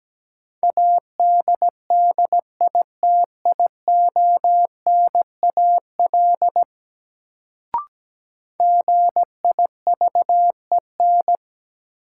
Morse code, heard as a keyed tone, about 17 wpm, 700 Hz. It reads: ADDITIONAL GIVEN
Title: Sentences from Top 1400 Words - New Word Drill Alt - 17wpm